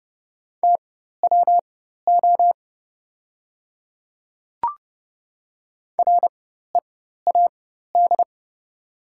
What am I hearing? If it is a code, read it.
Morse code, heard as a keyed tone, about 31 wpm, spaced as for 14 wpm, 690 Hz. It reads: TWO READ
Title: 4x Character Spacing - Top 200 Words - 30wpm